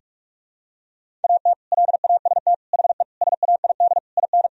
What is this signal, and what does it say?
Morse code, heard as a keyed tone, about 46 wpm, 700 Hz. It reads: AT LAST HE SAID IN
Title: Sentences from Top 1000 Words - Review Part 1 - 45wpm